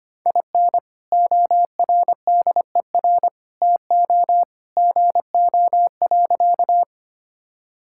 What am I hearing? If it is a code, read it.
Morse code, heard as a keyed tone, about 25 wpm, 700 Hz. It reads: IN ORDER TO GO.
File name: practice / 25wpm / Sentences with 2-Word Phrases - 25wpm